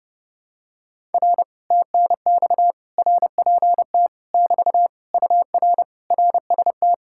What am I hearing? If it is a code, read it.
Morse code, heard as a keyed tone, about 30 wpm, 700 Hz. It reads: R TNX RPT <BT> UR RST